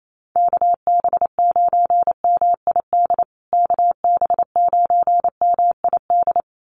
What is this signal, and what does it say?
Morse code, heard as a keyed tone, about 28 wpm, 700 Hz. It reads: KB9MID KB9MID
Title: USA Call Signs Repeated 2x 28wpm